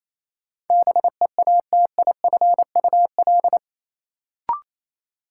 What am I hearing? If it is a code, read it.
Morse code, heard as a keyed tone, about 28 wpm, 700 Hz. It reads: BEATIFUL
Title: Sentences from Top 1300 Words - New Word Drill Alt - 28wpm